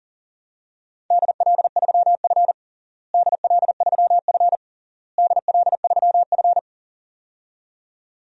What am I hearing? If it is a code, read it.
Morse code, heard as a keyed tone, about 37 wpm, 700 Hz. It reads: DL3F DL3F DL3F
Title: DXCC-100 Call Signs - 4 Characters - Repeated 3x - 40wpm